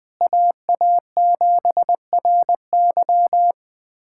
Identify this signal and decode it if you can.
Morse code, heard as a keyed tone, about 20 wpm, 700 Hz. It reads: AA7RY
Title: USA Callsign DE Callsign - 5 Characters - 20wpm